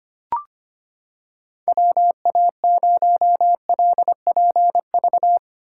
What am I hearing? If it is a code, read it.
Morse code, heard as a keyed tone, about 25 wpm, 700 Hz. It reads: WA0LPV
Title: K1USN SST QSO Practice - 25wpm